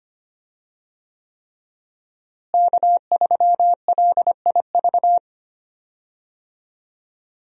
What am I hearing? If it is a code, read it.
Morse code, heard as a keyed tone, about 25 wpm, 700 Hz. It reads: K3LIV
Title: USA Call Signs - 5 Characters - Repeated 3x - 25wpm